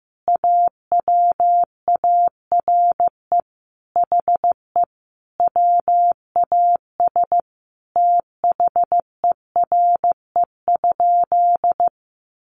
Morse code, heard as a keyed tone, about 14 wpm, 700 Hz. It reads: AWARE HE WAS THERE?
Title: Sentences from Top 1200 Words 15wpm